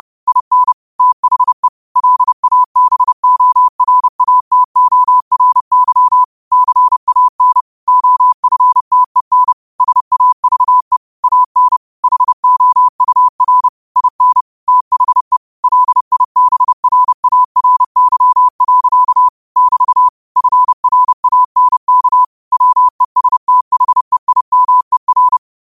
Morse code, heard as a keyed tone, about 30 wpm, 1000 Hz. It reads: IN THE LABORATORY CAN OFTEN SAVE AN HOUR IN THE LIBRARY. <BT> FRANK WESTHEIMER
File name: qotdfiles / 30wpm